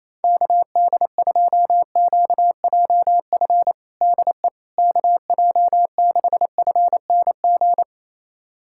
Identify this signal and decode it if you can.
Morse code, heard as a keyed tone, about 27 wpm, 700 Hz. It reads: KD2QJF DE KJ6FNG
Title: USA Callsign DE Callsign - 6 Characters - 28wpm